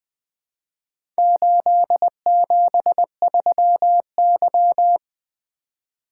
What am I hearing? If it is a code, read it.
Morse code, heard as a keyed tone, about 20 wpm, 700 Hz. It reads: 873Y
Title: Four Letter-Number Combinations - 20wpm